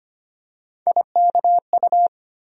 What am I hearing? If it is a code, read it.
Morse code, heard as a keyed tone, about 25 wpm, 700 Hz. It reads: IKU
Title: USA Call Sign Postfixes - 3 Letters 25wpm